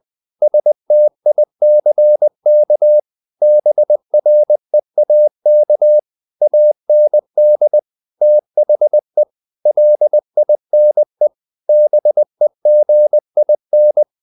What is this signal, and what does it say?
Morse code, heard as a keyed tone, about 20 wpm, 600 Hz. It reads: STICK BREAK AND THE LINE BEGIN
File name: the old man and the sea / the old man and the sea part 1 20wpm